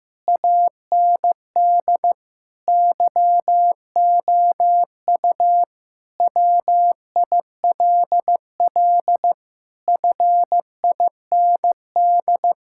Morse code, heard as a keyed tone, about 15 wpm, 700 Hz. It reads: AND YOU WILL FIND
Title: Sentences from Top 1600 Words 15wpm